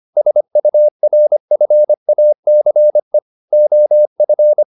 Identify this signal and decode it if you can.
Morse code, heard as a keyed tone, about 25 wpm, 600 Hz. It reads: SURFACE OF